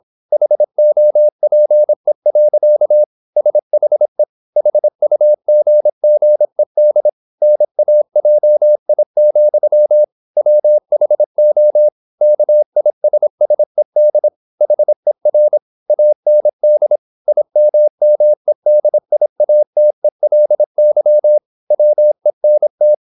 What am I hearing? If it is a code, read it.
Morse code, heard as a keyed tone, about 26 wpm, 600 Hz. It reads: HOPE. SHE HUGGED NAJI, WHO KISSED HER AND IMMEDIATELY WENT
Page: Never 26wpm | CW med Gnister